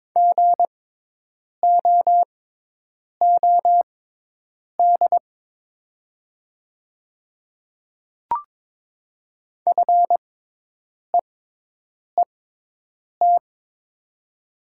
Morse code, heard as a keyed tone, about 22 wpm, spaced as for 8 wpm, 700 Hz. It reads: GOOD FEET
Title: 6x Character Spacing - Top 500 Words - 22wpm